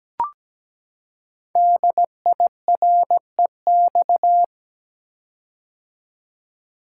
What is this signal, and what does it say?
Morse code, heard as a keyed tone, about 17 wpm, 700 Hz. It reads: DIREX
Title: Code Groups - 5 Letters - 17wpm